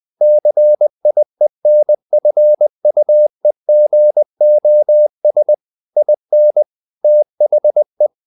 Morse code, heard as a keyed tone, about 20 wpm, 600 Hz. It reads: CIENFUEGOS IN THE
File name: the old man and the sea part 2 20wpm